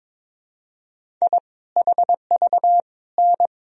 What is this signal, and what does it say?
Morse code, heard as a keyed tone, about 21 wpm, 700 Hz. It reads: I HV N
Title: Sentences from Top 400 Words and Abbreviations 22wpm